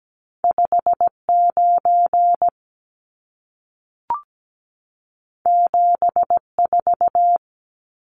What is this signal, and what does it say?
Morse code, heard as a keyed tone, about 17 wpm, 700 Hz. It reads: 59 74
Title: Numbers - 2-Digit 17wpm